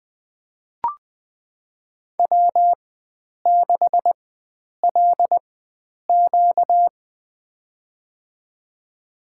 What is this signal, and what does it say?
Morse code, heard as a keyed tone, about 20 wpm, spaced as for 9 wpm, 700 Hz. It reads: W6LQ
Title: USA Call Signs - 4 Characters - 4x Character Spacing - 20wpm